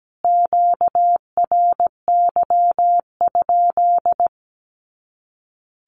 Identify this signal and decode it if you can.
Morse code, heard as a keyed tone, about 17 wpm, 700 Hz. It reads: QRY?
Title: Q-Codes Repeated 3x 17wpm